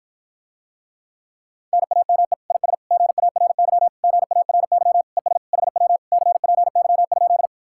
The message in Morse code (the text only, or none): NAME IS GARY GARY ES HW CPY?